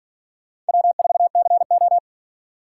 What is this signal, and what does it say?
Morse code, heard as a keyed tone, about 52 wpm, 700 Hz. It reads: W4CO